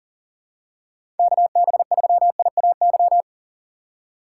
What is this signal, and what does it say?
Morse code, heard as a keyed tone, about 37 wpm, 700 Hz. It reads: KB3IAY
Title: USA Call Signs - 6 Characters - 4x Character Spacing - 40wpm